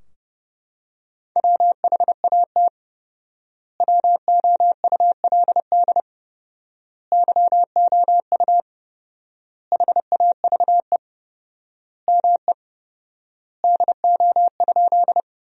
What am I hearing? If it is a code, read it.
Morse code, heard as a keyed tone, about 32 wpm, 700 Hz. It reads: WHAT WOULD YOU HAVE ME DO?
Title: 4x Word-Spacing - Sentences from Top 200 Words - 30wpm